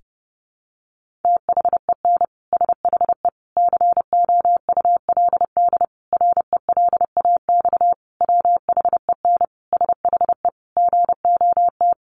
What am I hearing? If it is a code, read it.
Morse code, heard as a keyed tone, about 30 wpm, 700 Hz. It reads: THEN SHE COULD RELAX WHEN SHE GOT
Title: Sentences from Top 1400 Words 30wpm